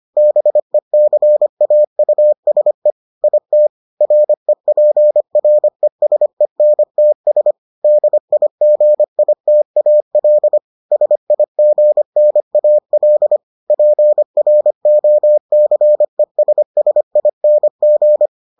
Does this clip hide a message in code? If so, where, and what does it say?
Yes: Morse code from 0.2 to 18.3 s, heard as a keyed tone, about 25 wpm, 600 Hz. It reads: BECAUSE IT REPRESENTS DIGITAL SIGNAL PROCESSING